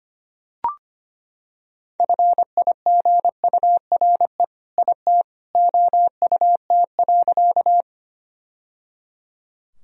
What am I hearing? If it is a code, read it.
Morse code, heard as a keyed tone, about 25 wpm, 700 Hz. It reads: FIGURE IT OUT.
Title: Sentences from Top 1000 Words - Review Part 2 - 25wpm